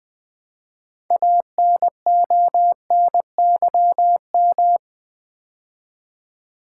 Morse code, heard as a keyed tone, about 20 wpm, 700 Hz. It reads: ANONYM
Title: Build A Word - 3 Letters At A Time - 9-Letter Words - 20wpm